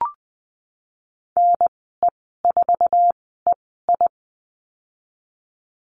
Morse code, heard as a keyed tone, about 19 wpm, spaced as for 14 wpm, 700 Hz. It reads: NE4EI
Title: 055 Lesson 10g - Letter r - Call Signs 20wpm